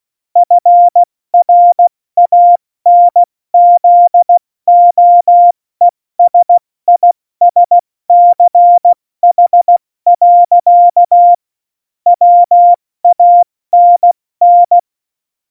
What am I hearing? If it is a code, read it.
Morse code, heard as a keyed tone, about 16 wpm, spaced as for 14 wpm, 700 Hz. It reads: FRANZOESISCH. WANN
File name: Morsekurs / Witze / Witz 20. November 2018, 70 F80